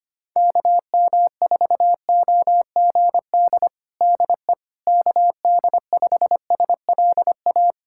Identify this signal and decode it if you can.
Morse code, heard as a keyed tone, about 25 wpm, 700 Hz. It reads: KM4OGD DE KD5SLA
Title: USA Callsign DE Callsign - 6 Characters - 25wpm